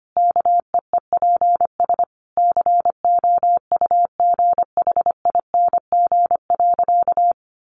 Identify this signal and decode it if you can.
Morse code, heard as a keyed tone, about 25 wpm, 700 Hz. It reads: KEEPS COUGHING.
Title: Additional Easy Phrases 25wpm